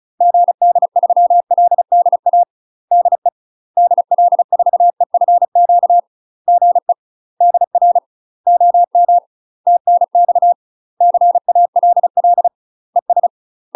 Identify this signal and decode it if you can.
Morse code, heard as a keyed tone, about 36 wpm, 700 Hz. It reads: GD3LDA DE DL4EFQ GE DR OM TNX CALL ES
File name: UK MASTER QSOs TRAINING / 35WPM / UK MASTER QSOs 0016